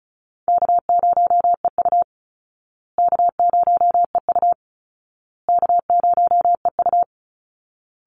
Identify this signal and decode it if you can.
Morse code, heard as a keyed tone, about 34 wpm, 700 Hz. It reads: K0EU K0EU K0EU
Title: Call Signs Repeated 2x - 35wpm